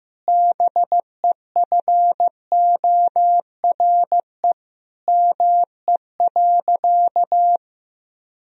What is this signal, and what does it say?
Morse code, heard as a keyed tone, about 16 wpm, 700 Hz. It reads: BEFORE ME.
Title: Sentences from Top 1000 Words - Review Part 1 - 15wpm